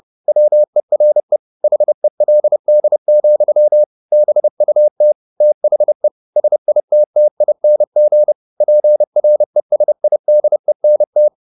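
Morse code, heard as a keyed tone, about 30 wpm, 600 Hz. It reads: WERE HELD, BUT THE SITTING PRESIDENT
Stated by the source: Morse code at 30 wpm, the novel Never by Ken Follett in English, from CW med Gnister.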